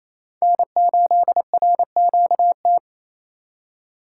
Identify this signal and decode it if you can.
Morse code, heard as a keyed tone, about 28 wpm, 700 Hz. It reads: N8RQT
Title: USA Call Signs - 2x Character Spacing - 28wpm